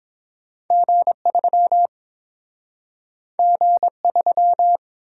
Morse code, heard as a keyed tone, about 30 wpm, 700 Hz. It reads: G3 G3